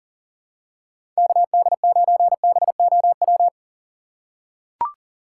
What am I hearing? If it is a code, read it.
Morse code, heard as a keyed tone, about 40 wpm, 700 Hz. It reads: KD9BOW
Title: USA Call Signs - 6 Characters - 3x Character Spacing - 40wpm